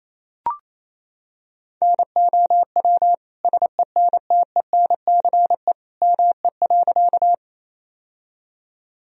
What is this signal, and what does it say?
Morse code, heard as a keyed tone, about 28 wpm, 700 Hz. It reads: NOW SENTENCE ME.
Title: Sentences from Top 1000 Words - Review Part 1 - 28wpm